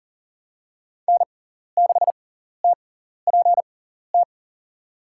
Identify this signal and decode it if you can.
Morse code, heard as a keyed tone, about 38 wpm, spaced as for 14 wpm, 700 Hz. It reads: N6TPT
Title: USA Call Signs - 6x Character Spacing - 40wpm